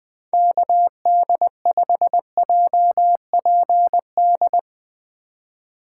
Morse code, heard as a keyed tone, about 21 wpm, 700 Hz. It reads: KD5JPD